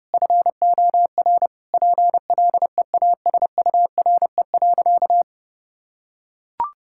Morse code, heard as a keyed tone, about 30 wpm, 700 Hz. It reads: FOR PLEASURE.
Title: Sentences from Top 1600 Words 30wpm